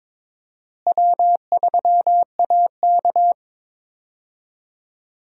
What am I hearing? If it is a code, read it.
Morse code, heard as a keyed tone, about 23 wpm, 700 Hz. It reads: W3AK
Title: USA Call Signs - 4 Characters - Repeated 2x - 22wpm